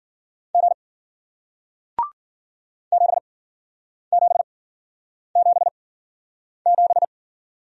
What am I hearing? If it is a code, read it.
Morse code, heard as a keyed tone, about 50 wpm, spaced as for 9 wpm, 700 Hz. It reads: G 7777